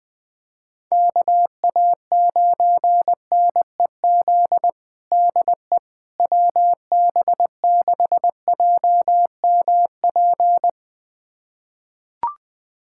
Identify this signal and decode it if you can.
Morse code, heard as a keyed tone, about 20 wpm, 700 Hz. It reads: KA9NEZ DE WB6JMP
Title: USA Callsign DE Callsign 20wpm